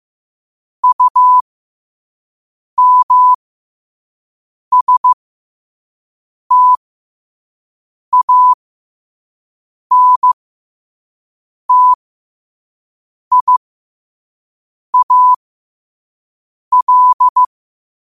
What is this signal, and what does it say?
Morse code, heard as a keyed tone, about 15 wpm, spaced as for 5 wpm, 1000 Hz. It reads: UMSTANTIAL
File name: qotdfiles / 5wpm